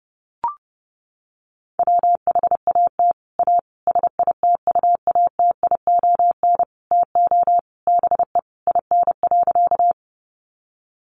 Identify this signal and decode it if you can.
Morse code, heard as a keyed tone, about 30 wpm, 700 Hz. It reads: WHAT A SITUATION TO BE IN.
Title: Sentences from Top 1200 Words 30wpm